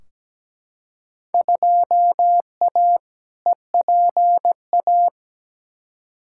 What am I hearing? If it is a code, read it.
Morse code, heard as a keyed tone, about 17 wpm, 700 Hz. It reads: 2A EPA
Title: Field Day Exchanges - 17wpm